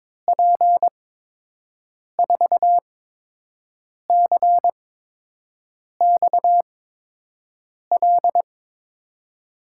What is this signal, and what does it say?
Morse code, heard as a keyed tone, about 22 wpm, spaced as for 6 wpm, 700 Hz. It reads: P4CXL
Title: USA Call Signs - 6 Characters - 8x Character Spacing - 22wpm